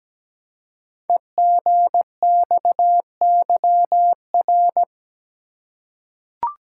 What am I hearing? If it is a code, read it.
Morse code, heard as a keyed tone, about 17 wpm, 700 Hz. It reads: EGXYR
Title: Code Groups - 5 Letters - 17wpm